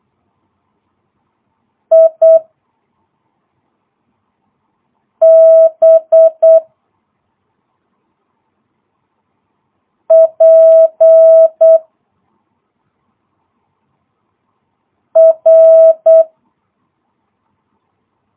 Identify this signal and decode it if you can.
Morse code, heard as a keyed tone, about 8 wpm, 650 Hz. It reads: I B P R
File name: jh8cbh / 2022cw